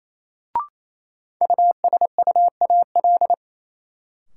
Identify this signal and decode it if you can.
Morse code, heard as a keyed tone, about 28 wpm, 700 Hz. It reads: USUAL
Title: Sentences from Top 400 Words - New Word Drill - 28wpm